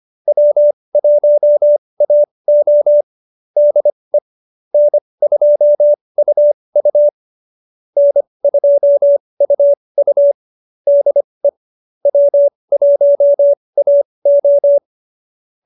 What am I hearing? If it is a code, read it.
Morse code, heard as a keyed tone, about 25 wpm, spaced as for 20 wpm, 590 Hz. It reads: W1AO DE N2UU N2UU DE W1AO